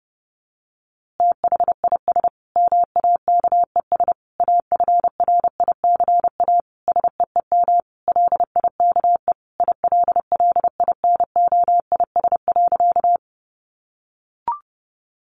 Morse code, heard as a keyed tone, about 30 wpm, 700 Hz. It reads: THIS MAKES AFRICA SEEM LIKE ILLINOIS.
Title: A Sound Of Thunder 30wpm